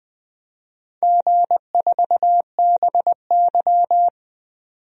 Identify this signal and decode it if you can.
Morse code, heard as a keyed tone, about 20 wpm, 700 Hz. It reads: G4BY
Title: DXCC-100 Call Signs - 4 Characters - 20wpm